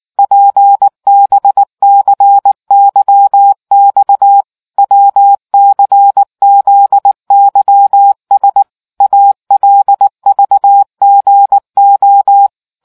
Morse code, heard as a keyed tone, about 19 wpm, 800 Hz. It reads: PBCYX WCZYS ALVGO